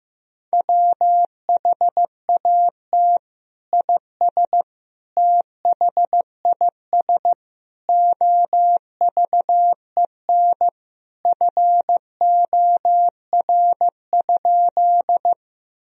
Morse code, heard as a keyed tone, about 15 wpm, 700 Hz. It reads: WHAT IS THIS OVEN FOR?
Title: Sentences from Top 1000 Words - Review Part 4 - 15wpm